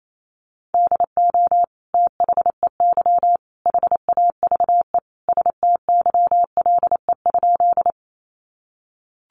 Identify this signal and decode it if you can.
Morse code, heard as a keyed tone, about 28 wpm, 700 Hz. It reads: DO THEY HAVE STYLE?
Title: Sentences from Top 800 Words 28wpm